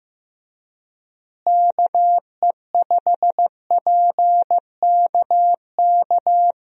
Morse code, heard as a keyed tone, about 16 wpm, 700 Hz. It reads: KE5PKK
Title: USA Call Signs - 6 Characters - Repeated 2x - 15wpm